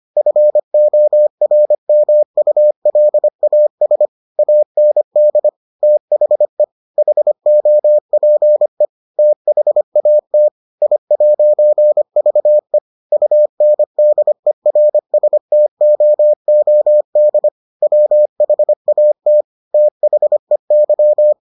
Morse code, heard as a keyed tone, about 25 wpm, 600 Hz. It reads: FORMULAS AND THE HOPE THAT I'VE UNDERSTOOD WHAT THEY